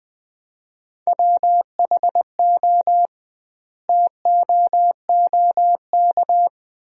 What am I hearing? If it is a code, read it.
Morse code, heard as a keyed tone, about 20 wpm, 700 Hz. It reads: WHO TOOK
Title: 2x Word-Spacing - Sentences from Top 300 Words - 20wpm